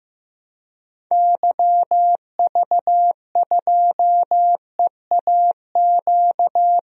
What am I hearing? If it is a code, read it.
Morse code, heard as a keyed tone, about 15 wpm, 700 Hz. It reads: YV2EAQ
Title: DXCC-100 Call Signs - 6 Characters - 15wpm